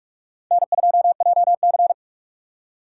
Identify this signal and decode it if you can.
Morse code, heard as a keyed tone, about 59 wpm, spaced as for 52 wpm, 700 Hz. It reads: N2JC
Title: Speed Racing - USA 1x2 Call Signs 45wpm